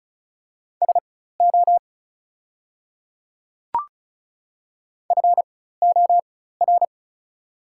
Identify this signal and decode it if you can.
Morse code, heard as a keyed tone, about 36 wpm, spaced as for 16 wpm, 690 Hz. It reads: SO FOR
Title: 4x Character Spacing - Top 100 Words - 35wpm